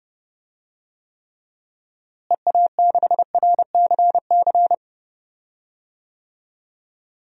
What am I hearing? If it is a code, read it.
Morse code, heard as a keyed tone, about 32 wpm, 700 Hz. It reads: EA6RCC
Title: DXCC-100 Call Signs - 6 Characters - Repeated 3x - 30wpm